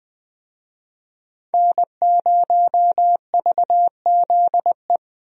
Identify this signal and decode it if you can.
Morse code, heard as a keyed tone, about 20 wpm, 700 Hz. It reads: N0VZE
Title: USA Call Signs - 5 Characters - Repeated 3x - 20wpm